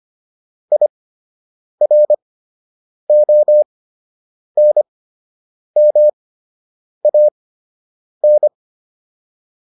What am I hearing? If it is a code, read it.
Morse code, heard as a keyed tone, about 25 wpm, spaced as for 8 wpm, 600 Hz. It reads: IRONMAN